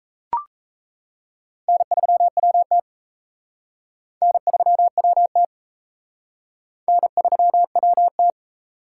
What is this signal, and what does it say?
Morse code, heard as a keyed tone, about 33 wpm, 700 Hz. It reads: N3WT N3WT N3WT
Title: Speed Racing - USA 1x2 Call Signs 28wpm